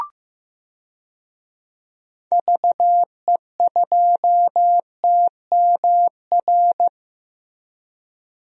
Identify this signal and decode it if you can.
Morse code, heard as a keyed tone, about 15 wpm, 700 Hz. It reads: VE2TMR
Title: DXCC-100 Call Signs - 6 Characters - Repeated 3x - 15wpm